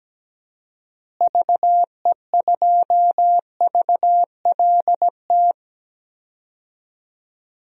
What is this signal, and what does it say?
Morse code, heard as a keyed tone, about 17 wpm, 700 Hz. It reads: VE2VLT
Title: DXCC-100 Call Signs - 6 Characters - Repeated 2x - 17wpm